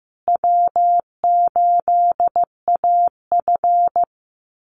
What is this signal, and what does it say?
Morse code, heard as a keyed tone, about 15 wpm, 700 Hz. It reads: W8AF
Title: USA Call Signs Repeated 2x 15wpm